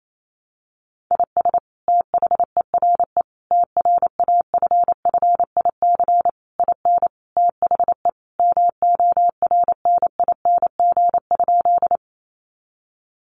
Morse code, heard as a keyed tone, about 28 wpm, 700 Hz. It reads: IS THERE TRAFFIC IN THE MORNING?
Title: Sentences from Top 1300 Words 28wpm